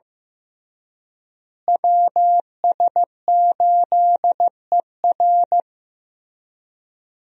Morse code, heard as a keyed tone, about 15 wpm, 700 Hz. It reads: WS8ER
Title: USA Call Signs - 5 Characters - Repeated 3x - 15wpm